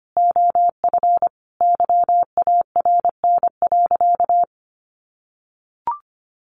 Morse code, heard as a keyed tone, about 25 wpm, 700 Hz. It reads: OF YARN.
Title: Additional Easy Phrases 25wpm